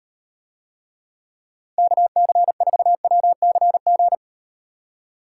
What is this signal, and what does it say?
Morse code, heard as a keyed tone, about 38 wpm, 700 Hz. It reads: KC4WCG